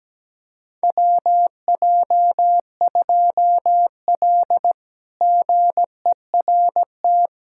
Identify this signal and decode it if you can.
Morse code, heard as a keyed tone, about 17 wpm, 700 Hz. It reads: WJ2L GERT
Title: K1USN SST QSO Practice - 17wpm